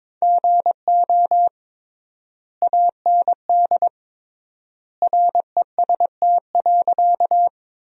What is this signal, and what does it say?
Morse code, heard as a keyed tone, about 22 wpm, 700 Hz. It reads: GO AND REST.
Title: 3x Word-Spacing - Sentences from Top 500 Words - 22wpm